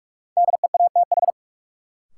Morse code, heard as a keyed tone, about 44 wpm, 700 Hz. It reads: DEATH
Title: Sentences from Top 1100 Words - New Word Drill Alt - 45wpm